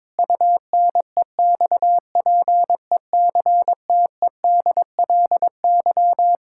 Morse code, heard as a keyed tone, about 21 wpm, 700 Hz. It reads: UNEXPECTEDLY
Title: Pangram Challenge v2 - 22wpm